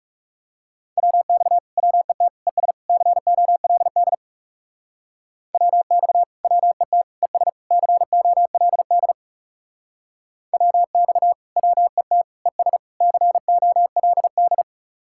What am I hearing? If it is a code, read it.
Morse code, heard as a keyed tone, about 40 wpm, 700 Hz. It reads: WX WET ES COLD WX WET ES COLD WX WET ES COLD